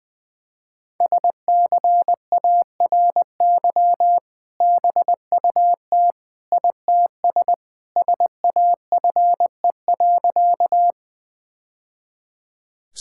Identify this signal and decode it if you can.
Morse code, heard as a keyed tone, about 21 wpm, 700 Hz. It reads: SCARY BUT ITS SAFE.
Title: Sentences with 2-Word Phrases - 20wpm